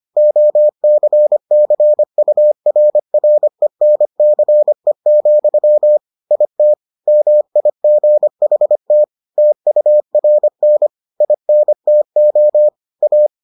Morse code, heard as a keyed tone, about 25 wpm, 600 Hz. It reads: OCCURRENCE, IT MIGHT TURN INTO A